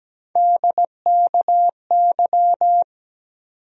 Morse code, heard as a keyed tone, about 17 wpm, 700 Hz. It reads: DKY
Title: USA Call Sign Postfixes - 3 Letters 17wpm